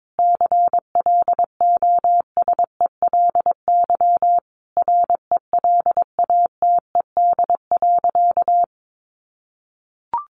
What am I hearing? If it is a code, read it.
Morse code, heard as a keyed tone, about 22 wpm, 700 Hz. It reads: CLOSELY RELATED.
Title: Sentences from Top 1400 Words 22wpm